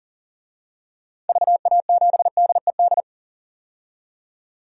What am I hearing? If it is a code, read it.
Morse code, heard as a keyed tone, about 40 wpm, 700 Hz. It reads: VA7DED
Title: DXCC-100 Call Signs - 6 Characters - 40wpm